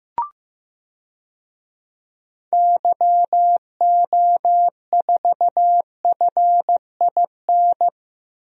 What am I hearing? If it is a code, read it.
Morse code, heard as a keyed tone, about 15 wpm, 700 Hz. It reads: YO4FIN
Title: DXCC-100 Call Signs - 6 Characters - 15wpm